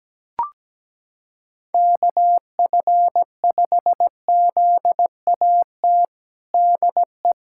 Morse code, heard as a keyed tone, about 17 wpm, 700 Hz. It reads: KF5ZAT DE
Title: USA Callsign DE Callsign - 6 Characters - 17wpm